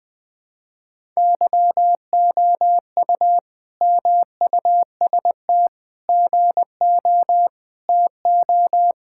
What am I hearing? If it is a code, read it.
Morse code, heard as a keyed tone, about 21 wpm, 700 Hz. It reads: YOU MUST GO TO
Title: Sentences from Top 1000 Words - Review Part 5 - 20wpm